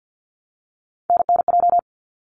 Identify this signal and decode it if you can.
Morse code, heard as a keyed tone, about 50 wpm, 700 Hz. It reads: NNJ
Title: ARRL Sections 50wpm